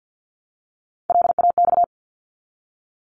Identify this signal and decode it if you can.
Morse code, heard as a keyed tone, about 50 wpm, 700 Hz. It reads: LAX
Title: ARRL Sections 50wpm